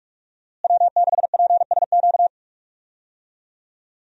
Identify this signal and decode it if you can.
Morse code, heard as a keyed tone, about 46 wpm, 700 Hz. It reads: W6PSQ
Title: USA Call Signs - 5 Characters - Repeated 3x - 45wpm